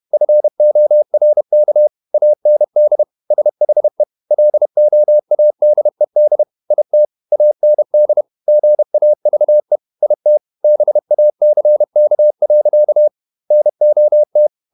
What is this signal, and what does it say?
Morse code, heard as a keyed tone, about 31 wpm, 600 Hz. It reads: FORK AND SHE LOADED IT AND GAVE IT BACK. NOT
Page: Never 31wpm | CW med Gnister